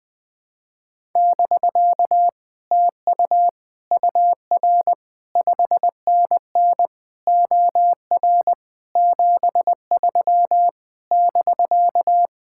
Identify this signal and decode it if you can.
Morse code, heard as a keyed tone, about 20 wpm, 700 Hz. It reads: <BK> TU UR 5NN OR 73 <BK>
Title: Speed-Racing Common QSO Phrases Encore - 17wpm